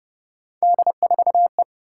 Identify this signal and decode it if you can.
Morse code, heard as a keyed tone, about 28 wpm, 700 Hz. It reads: D4E